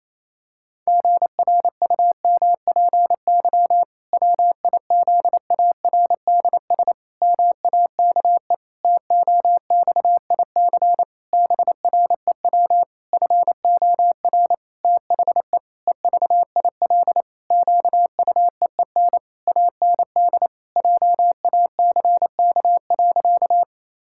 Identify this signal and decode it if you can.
Morse code, heard as a keyed tone, about 28 wpm, 700 Hz. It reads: GRUMPY WIZARDS MAKE TOXIC BREW FOR THE EVIL QUEEN AND JACK.
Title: Pangram Challenge v2 - 28wpm